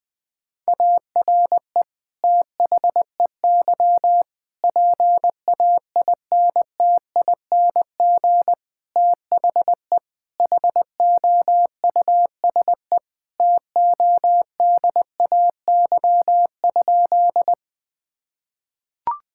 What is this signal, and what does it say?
Morse code, heard as a keyed tone, about 21 wpm, 700 Hz. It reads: ARE THEY PAINTING THE HOUSE TODAY?
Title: Sentences from Top 1000 Words - Review Part 6 - 20wpm